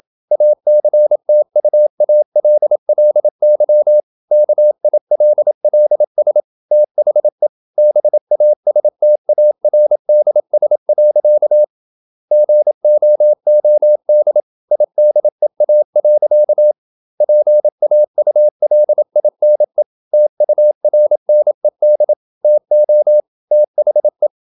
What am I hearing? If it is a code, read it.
Morse code, heard as a keyed tone, about 27 wpm, 600 Hz. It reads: ACTUALLY KILLS THE BASTARDS. GOOD IDEA. PAULINE TURNED TO THE
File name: Never 0027wpm